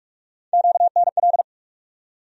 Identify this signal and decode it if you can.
Morse code, heard as a keyed tone, about 46 wpm, 700 Hz. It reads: QNL
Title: USA Call Sign Postfixes - 3 Letters 45wpm